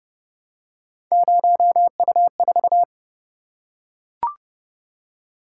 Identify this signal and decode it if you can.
Morse code, heard as a keyed tone, about 29 wpm, 700 Hz. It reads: 0U4